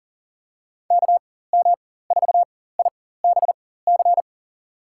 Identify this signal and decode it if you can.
Morse code, heard as a keyed tone, about 38 wpm, spaced as for 18 wpm, 700 Hz. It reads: KM4IBC
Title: USA Call Signs - 6 Characters - 4x Character Spacing - 40wpm